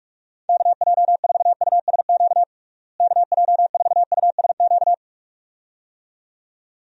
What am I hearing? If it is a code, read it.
Morse code, heard as a keyed tone, about 42 wpm, 700 Hz. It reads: KJ4USQ KJ4USQ
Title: USA Call Signs - 6 Characters - Repeated 2x - 45wpm